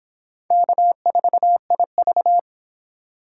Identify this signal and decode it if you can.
Morse code, heard as a keyed tone, about 24 wpm, 700 Hz. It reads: K4IV